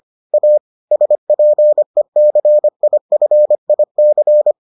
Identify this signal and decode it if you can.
Morse code, heard as a keyed tone, about 25 wpm, 600 Hz. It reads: A SPECIFIC